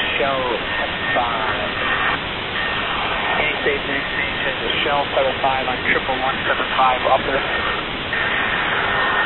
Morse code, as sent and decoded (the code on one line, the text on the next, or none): none